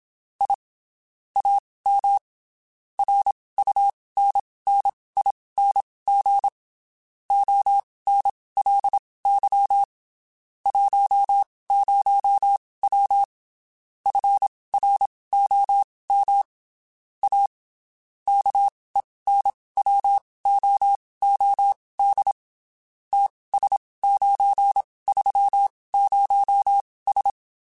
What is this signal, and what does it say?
Morse code, heard as a keyed tone, about 27 wpm, spaced as for 19 wpm, 800 Hz. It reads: I AM RUNNING ONLY 10W FROM A KENWOOD TS930S
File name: (spaced) / QSOs Spaced-mp3s / kc7haa @25 sp